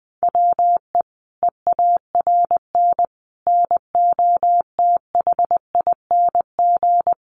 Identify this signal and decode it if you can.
Morse code, heard as a keyed tone, about 20 wpm, 700 Hz. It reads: WE EARN NOTHING
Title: Sentences from Top 1500 Words 20wpm